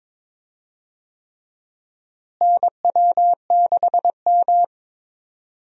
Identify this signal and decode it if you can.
Morse code, heard as a keyed tone, about 21 wpm, 700 Hz. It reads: NW6M
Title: USA Call Signs - 4 Characters - Repeated 3x - 22wpm